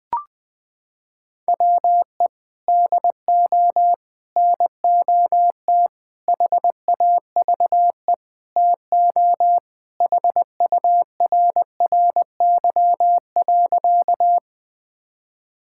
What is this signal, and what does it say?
Morse code, heard as a keyed tone, about 21 wpm, 700 Hz. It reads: WE DO NOT HAVE TO HURRY.
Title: Sentences with 5-Word Phrases - 20wpm